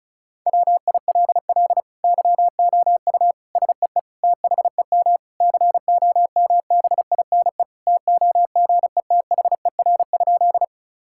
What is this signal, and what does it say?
Morse code, heard as a keyed tone, about 34 wpm, 700 Hz. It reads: WILL YOU SEE THEM COMBINE TOGETHER?
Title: Sentences from Top 1600 Words 35wpm